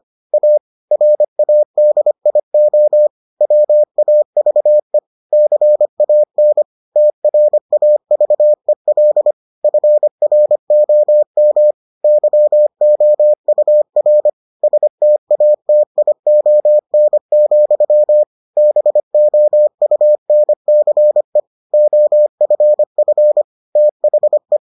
Morse code, heard as a keyed tone, about 25 wpm, 600 Hz. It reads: A RADIO WAVE CAN TRAVEL FROM YOUR STATION, BOUNCE OFF THE